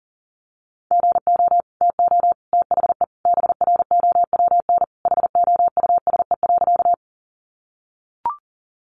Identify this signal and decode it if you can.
Morse code, heard as a keyed tone, about 40 wpm, 700 Hz. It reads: GO TO THE BROWN HOUSE.
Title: Sentences from Top 1200 Words 40wpm